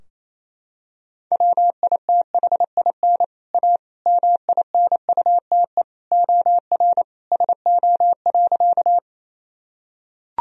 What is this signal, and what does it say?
Morse code, heard as a keyed tone, about 28 wpm, 700 Hz. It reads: WITHIN A MINUTE OR SO.
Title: Sentences from Top 1000 Words - Review Part 4 - 28wpm